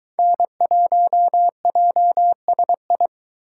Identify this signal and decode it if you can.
Morse code, heard as a keyed tone, about 22 wpm, 700 Hz. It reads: N1JSI